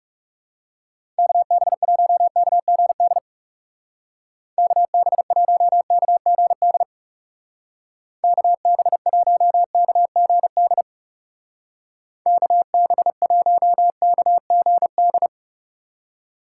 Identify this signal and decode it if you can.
Morse code, heard as a keyed tone, about 36 wpm, 700 Hz. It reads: KB1KGD KB1KGD KB1KGD KB1KGD
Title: Speed Racing - USA 2x3 Call Signs 30wpm